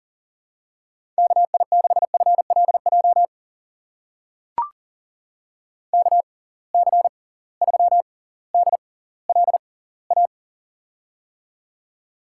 Morse code, heard as a keyed tone, about 40 wpm, spaced as for 20 wpm, 700 Hz. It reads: KI6FLJ KC3DLA
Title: USA Call Signs - 6 Characters - 6x Character Spacing - 40wpm